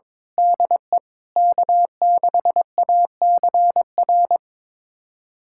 Morse code, heard as a keyed tone, about 22 wpm, 700 Hz. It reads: DE K6ACR
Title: USA Callsign DE Callsign - 5 Characters - 22wpm